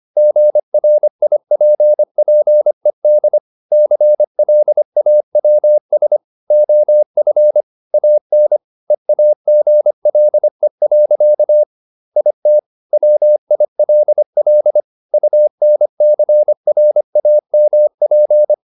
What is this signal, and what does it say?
Morse code, heard as a keyed tone, about 25 wpm, 600 Hz. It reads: GRIPPED CLAWS OF AN EAGLE. IT WILL UNCRAMP
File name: the old man and the sea part 1 25wpm